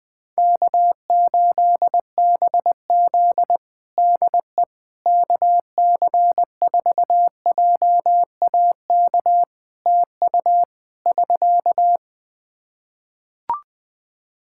Morse code, heard as a keyed tone, about 19 wpm, 700 Hz. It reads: K8BZ DE KC4JAK TU <SK>
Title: Practice QSO - Exchange 4 - Ending - 20wpm